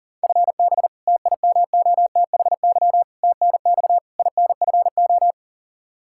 Morse code, heard as a keyed tone, about 37 wpm, 690 Hz. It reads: FB TIMOTHY TNX INFO
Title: Practice QSO - Exchange 3 - More Info - 40wpm